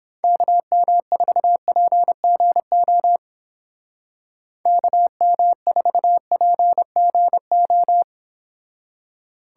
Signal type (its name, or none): Morse code